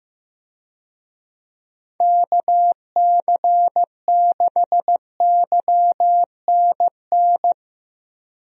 Morse code, heard as a keyed tone, about 15 wpm, 700 Hz. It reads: KC6YNN
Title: USA Call Signs - 6 Characters - 15wpm